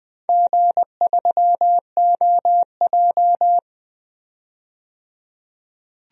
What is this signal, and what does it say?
Morse code, heard as a keyed tone, about 20 wpm, 700 Hz. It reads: G3OJ
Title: DXCC-100 Call Signs - 4 Characters - 20wpm